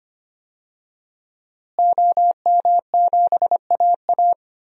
Morse code, heard as a keyed tone, about 25 wpm, 700 Hz. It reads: OM7AA
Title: DXCC-100 Call Signs - 5 Characters - Repeated 3x - 25wpm